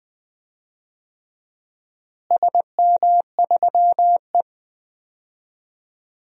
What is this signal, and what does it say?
Morse code, heard as a keyed tone, about 20 wpm, 700 Hz. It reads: SM3E
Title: DXCC-100 Call Signs - 4 Characters - Repeated 3x - 20wpm